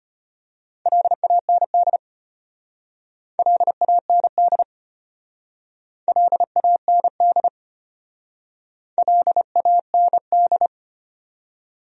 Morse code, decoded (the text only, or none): LAND LAND LAND LAND